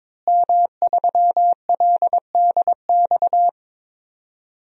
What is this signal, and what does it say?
Morse code, heard as a keyed tone, about 22 wpm, 700 Hz. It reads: M3LDX
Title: DXCC-100 Call Signs - 5 Characters - Repeated 2x - 22wpm